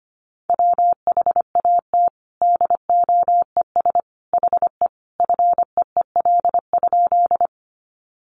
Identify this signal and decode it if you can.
Morse code, heard as a keyed tone, about 25 wpm, 700 Hz. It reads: WHAT DOES HE FEEL?
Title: Sentences from Top 400 Words 25wpm